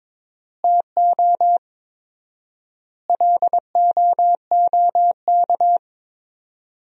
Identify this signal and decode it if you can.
Morse code, heard as a keyed tone, about 21 wpm, 700 Hz. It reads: TO LOOK
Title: 4x Word-Spacing - Sentences from Top 100 Words - 22wpm